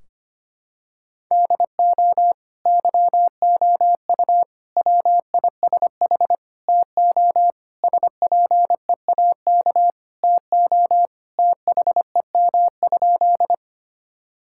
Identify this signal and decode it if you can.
Morse code, heard as a keyed tone, about 25 wpm, 700 Hz. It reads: DO YOU WISH TO SPEAK TO THEM?
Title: Sentences from Top 1600 Words 25wpm